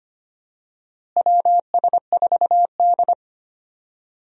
Morse code, heard as a keyed tone, about 25 wpm, 700 Hz. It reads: WS4D
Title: USA Call Signs - 4 Characters - Repeated 2x - 25wpm